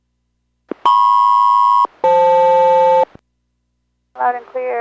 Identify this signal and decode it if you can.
Aircraft SELCAL, heard as two pairs of tones, with speech. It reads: MP-FL